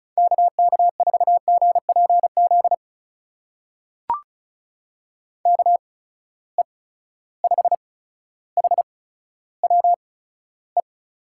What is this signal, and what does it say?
Morse code, heard as a keyed tone, about 36 wpm, spaced as for 15 wpm, 700 Hz. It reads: KK4GPZ KE5HWE
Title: USA Call Signs - 6 Characters - 8x Character Spacing - 35wpm